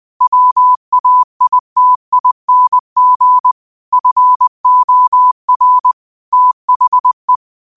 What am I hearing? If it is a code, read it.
Morse code, heard as a keyed tone, about 20 wpm, 1000 Hz. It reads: WAITING FOR THE
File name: qotdfiles / 20wpm